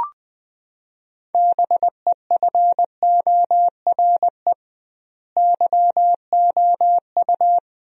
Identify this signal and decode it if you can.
Morse code, heard as a keyed tone, about 20 wpm, 700 Hz. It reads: BEFORE YOU
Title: 2x Word-Spacing - Sentences from Top 400 Words - 20wpm